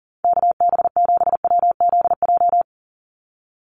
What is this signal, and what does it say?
Morse code, heard as a keyed tone, about 40 wpm, 690 Hz. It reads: KB7WZJ
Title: USA Call Signs 40wpm